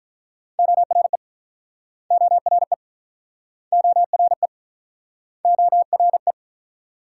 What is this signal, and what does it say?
Morse code, heard as a keyed tone, about 47 wpm, 700 Hz. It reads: ORE ORE ORE ORE